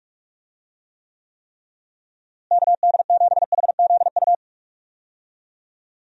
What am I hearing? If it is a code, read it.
Morse code, heard as a keyed tone, about 46 wpm, 700 Hz. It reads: KD7HZU
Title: USA Call Signs - 6 Characters - Repeated 3x - 45wpm